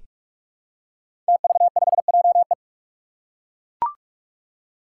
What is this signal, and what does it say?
Morse code, heard as a keyed tone, about 43 wpm, 700 Hz. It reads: TV5JE
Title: Five Challenging Letter-Number Combinations 45wpm